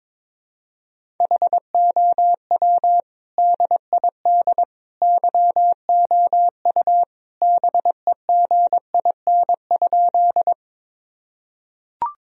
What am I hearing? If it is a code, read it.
Morse code, heard as a keyed tone, about 22 wpm, 700 Hz. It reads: HOW DID YOU BEGIN?
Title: Sentences from Top 1000 Words - Review Part 1 - 22wpm